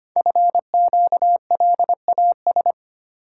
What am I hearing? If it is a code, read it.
Morse code, heard as a keyed tone, about 25 wpm, 700 Hz. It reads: FQLAS
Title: Code Groups - 5 Letters - 25wpm